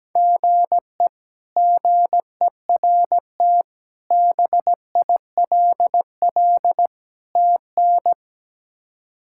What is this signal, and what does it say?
Morse code, heard as a keyed tone, about 17 wpm, 700 Hz. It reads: GE GERT BILL TN
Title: K1USN SST QSO Practice - 17wpm